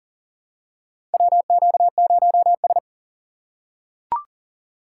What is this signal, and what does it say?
Morse code, heard as a keyed tone, about 40 wpm, 700 Hz. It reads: WQ0S
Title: USA Call Signs - 4 Characters - 4x Character Spacing - 40wpm